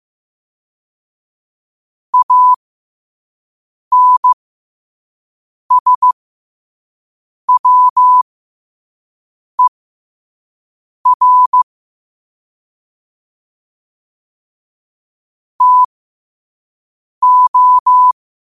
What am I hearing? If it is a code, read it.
Morse code, heard as a keyed tone, about 15 wpm, spaced as for 5 wpm, 1000 Hz. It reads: ANSWER TO
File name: 5wpm